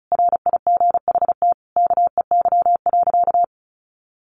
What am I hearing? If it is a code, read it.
Morse code, heard as a keyed tone, about 34 wpm, 700 Hz. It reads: RIGHT KEY.
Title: Sentences from Top 600 Words 35wpm